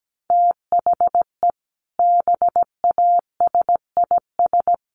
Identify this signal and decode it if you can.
Morse code, heard as a keyed tone, about 17 wpm, 700 Hz. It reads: THE BASIS
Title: Sets of 3 Words 17wpm